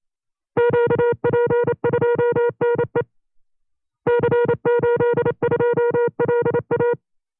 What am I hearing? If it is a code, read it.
Morse code, heard as a keyed tone, about 28 wpm, 480 Hz. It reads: QP2NE C82LA
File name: FreeFiles / CW-practice / 25WPM-side2